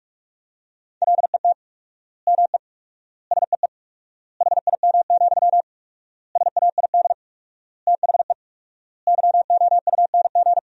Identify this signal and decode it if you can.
Morse code, heard as a keyed tone, about 46 wpm, 700 Hz. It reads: LET ME SEE HIM, SAID THE YOUNG
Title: 4x Word-Spacing - Sentences from Top 400 Words - 45wpm